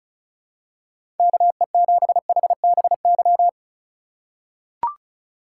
Morse code, heard as a keyed tone, about 36 wpm, 700 Hz. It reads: KE7HBY
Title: USA Call Signs - 6 Characters - 4x Character Spacing - 35wpm